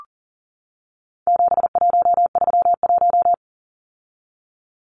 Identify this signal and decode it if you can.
Morse code, heard as a keyed tone, about 40 wpm, 700 Hz. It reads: 7131
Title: Numbers - 4-Digit 40wpm